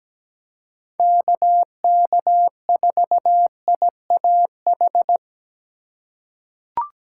Morse code, heard as a keyed tone, about 17 wpm, 700 Hz. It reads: KK4IAH
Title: USA Call Signs - 6 Characters - 3x Character Spacing - 17wpm